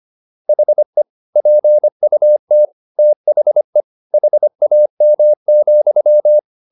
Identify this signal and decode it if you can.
Morse code, heard as a keyed tone, about 25 wpm, 600 Hz. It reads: HE PUT THE HAM,